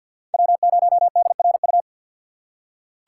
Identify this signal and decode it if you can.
Morse code, heard as a keyed tone, about 47 wpm, 700 Hz. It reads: W0DRU
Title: USA Call Signs - 6x Character Spacing - 50wpm